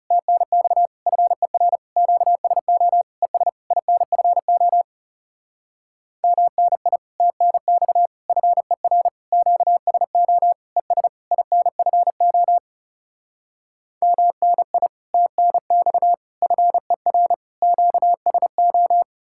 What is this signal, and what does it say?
Morse code, heard as a keyed tone, about 37 wpm, 700 Hz. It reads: TNX FER QSO ES INFO MNI TNX FER QSO ES INFO MNI TNX FER QSO
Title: Speed-Racing Common QSO Phrases - 30wpm